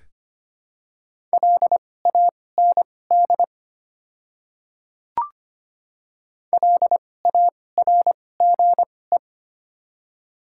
Morse code, heard as a keyed tone, about 25 wpm, spaced as for 18 wpm, 700 Hz. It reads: LAND LARGE
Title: 2x Character Spacing - Top 200 Words - 25wpm